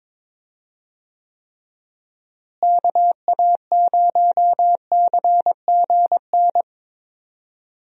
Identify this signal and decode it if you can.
Morse code, heard as a keyed tone, about 22 wpm, 700 Hz. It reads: KA0CGN
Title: USA Call Signs - 6 Characters - Repeated 3x - 22wpm